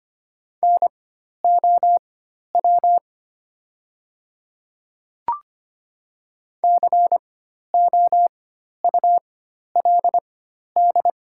Morse code, heard as a keyed tone, about 25 wpm, spaced as for 12 wpm, 700 Hz. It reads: NOW COULD
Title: 4x Character Spacing - Top 100 Words - 25wpm